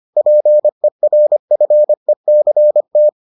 Morse code, heard as a keyed tone, about 25 wpm, 600 Hz. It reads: PERFECT